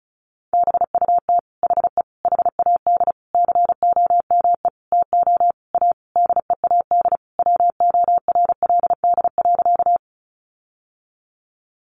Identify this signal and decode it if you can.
Morse code, heard as a keyed tone, about 34 wpm, 700 Hz. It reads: BUT HE HAD COME TO A DEAD WORLD.
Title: Sentences from Top 1500 Words 35wpm